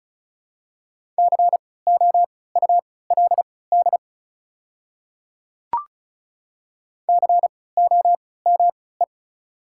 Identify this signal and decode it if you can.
Morse code, heard as a keyed tone, about 36 wpm, spaced as for 20 wpm, 700 Hz. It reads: COULD COME
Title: 3x Character Spacing - Top 100 Words - 35wpm